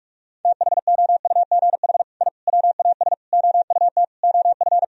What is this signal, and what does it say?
Morse code, heard as a keyed tone, about 41 wpm, 700 Hz. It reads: THOUGH I WAS OUT OF
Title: Sentences from Top 1000 Words - Review Part 4 - 45wpm